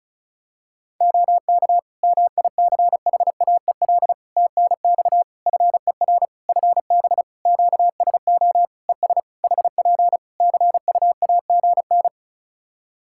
Morse code, heard as a keyed tone, about 34 wpm, 700 Hz. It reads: OK MICHAEL TNX FER FB QSO ES HP CUAGN
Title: K1USN SST QSO Practice 35wpm